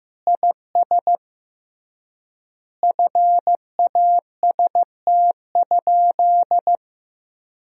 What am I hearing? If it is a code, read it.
Morse code, heard as a keyed tone, about 15 wpm, 700 Hz. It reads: IS FAST?
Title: 3x Word-Spacing - Sentences from Top 400 Words - 15wpm